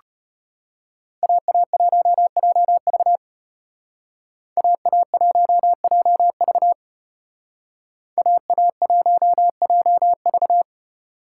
Morse code, heard as a keyed tone, about 34 wpm, 700 Hz. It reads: AA1JV AA1JV AA1JV